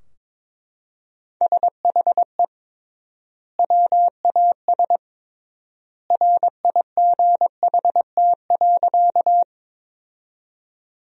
Morse code, heard as a keyed tone, about 22 wpm, 700 Hz. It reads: SHE WAS RIGHT.
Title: 3x Word-Spacing - Sentences from Top 200 Words - 22wpm